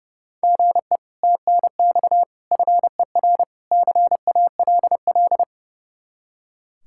Morse code, heard as a keyed tone, about 38 wpm, 700 Hz. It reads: GE TNX FER CALL